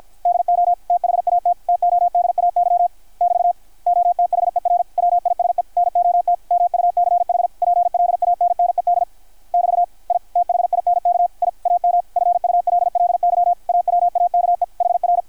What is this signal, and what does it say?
Morse code, heard as a keyed tone, about 52 wpm, 660 Hz, with radio noise. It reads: DO THAT TODAY <BT> OTHERWISE NOT MUCH PLANNED <BT> I THINK I AM FULLY AWAKE HR
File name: archived / CRSnet